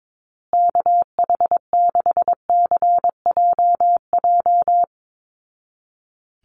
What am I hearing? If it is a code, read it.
Morse code, heard as a keyed tone, about 22 wpm, 700 Hz. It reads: KH6CJJ
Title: Call Signs Repeated 2x - 22wpm